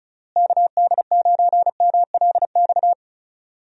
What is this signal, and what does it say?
Morse code, heard as a keyed tone, about 36 wpm, 700 Hz. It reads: KD9MLX
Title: USA Call Signs - 6 Characters - Repeated 3x - 35wpm